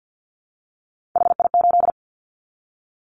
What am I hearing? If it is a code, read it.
Morse code, heard as a keyed tone, about 50 wpm, 700 Hz. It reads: HI8
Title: Callsign Prefixes for Top 100 DXCC - 50wpm